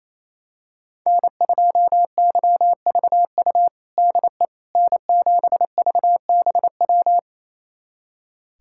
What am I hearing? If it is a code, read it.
Morse code, heard as a keyed tone, about 28 wpm, 700 Hz. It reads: N2YVU DE N7VBW
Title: USA Callsign DE Callsign 28wpm